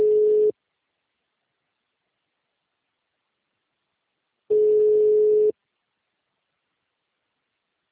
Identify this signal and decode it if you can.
Ringback tone (Europe, 425 Hz).